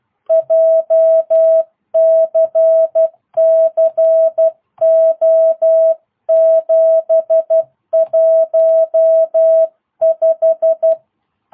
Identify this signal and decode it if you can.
Morse code, heard as a keyed tone, about 12 wpm, 660 Hz. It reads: JCC O715